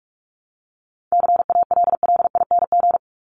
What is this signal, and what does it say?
Morse code, heard as a keyed tone, about 44 wpm, 700 Hz. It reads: CALLING
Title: ING Suffix - Top 100 Words with Base Words 45wpm